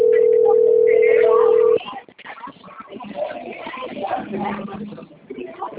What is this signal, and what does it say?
Ringback tone (North America), 440 + 480 Hz, with speech.
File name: rg-1600-unknown- - - .46354